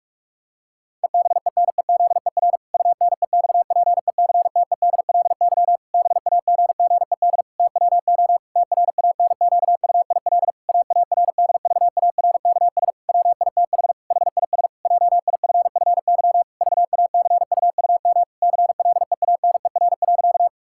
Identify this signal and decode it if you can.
Morse code, heard as a keyed tone, about 44 wpm, 700 Hz. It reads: EBENEZER UNEXPECTEDLY BAGGED TWO TRANQUIL AARDVARKS WITH HIS JIFFY VACUUM CLEANER.
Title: Pangram Challenge v2 - 45wpm